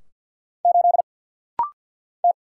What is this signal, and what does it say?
Morse code, heard as a keyed tone, about 48 wpm, 690 Hz. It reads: 8 T
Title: Single Letter-Number - Rapid-Fire 50wpm